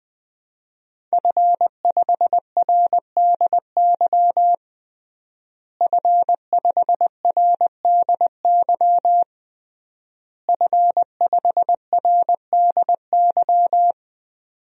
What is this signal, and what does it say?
Morse code, heard as a keyed tone, about 20 wpm, 700 Hz. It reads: F5RDY F5RDY F5RDY
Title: DXCC-100 Call Signs - 5 Characters - Repeated 3x - 20wpm